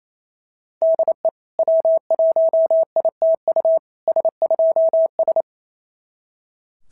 Morse code, heard as a keyed tone, about 25 wpm, 650 Hz. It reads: DE W1ITU S2S
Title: SOTA QSO Practice 28wpm